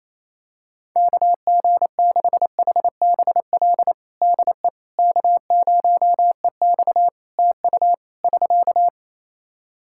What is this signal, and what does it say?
Morse code, heard as a keyed tone, about 28 wpm, 700 Hz. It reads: KG6HBL DE K0EX TU <SK>
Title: K1USN SST QSO Practice 28wpm